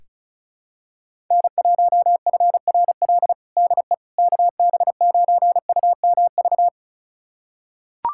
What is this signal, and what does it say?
Morse code, heard as a keyed tone, about 34 wpm, 700 Hz. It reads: N1FRL DE KB9UMV
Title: USA Callsign DE Callsign 35wpm